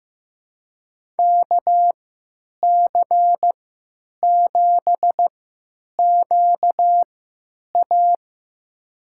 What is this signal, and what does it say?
Morse code, heard as a keyed tone, about 15 wpm, spaced as for 8 wpm, 700 Hz. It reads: KC7QA
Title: USA Call Signs - 5 Characters - 3x Character Spacing - 15wpm